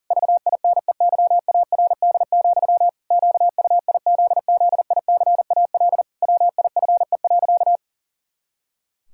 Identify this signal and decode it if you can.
Morse code, heard as a keyed tone, about 37 wpm, 700 Hz. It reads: VINEYARD, QUIZZICAL WIFE.
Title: Pangram Challenge v2 - 40wpm